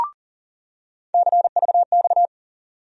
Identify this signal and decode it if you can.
Morse code, heard as a keyed tone, about 40 wpm, 700 Hz. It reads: CVX
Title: USA Call Sign Postfixes - 3 Letters 40wpm